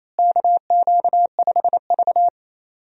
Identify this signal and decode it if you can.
Morse code, heard as a keyed tone, about 28 wpm, 700 Hz. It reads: KQ5V
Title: USA Call Signs - 4 Characters - Repeated 2x - 28wpm